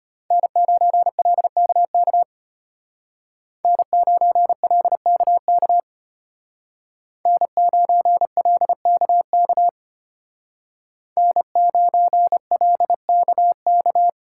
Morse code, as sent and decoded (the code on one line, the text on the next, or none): -. ----. .-.. -.- -.- / -. ----. .-.. -.- -.- / -. ----. .-.. -.- -.- / -. ----. .-.. -.- -.-
N9LKK N9LKK N9LKK N9LKK